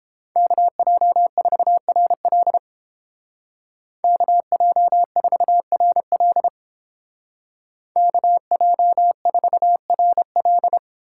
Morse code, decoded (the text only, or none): KJ4RL KJ4RL KJ4RL